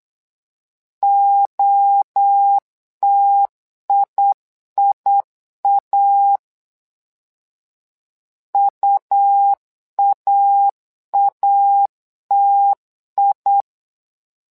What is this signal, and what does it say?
Morse code, heard as a keyed tone, about 8 wpm, 800 Hz. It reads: OTIIA UAATI
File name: LEZIONI / 3 MWRO / 3 MWRO 8wpm 2606